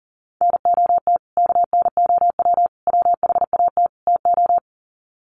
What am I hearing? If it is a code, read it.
Morse code, heard as a keyed tone, about 40 wpm, 700 Hz. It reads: NOT KNOW WHAT TO
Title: Sets of 4 Words 40wpm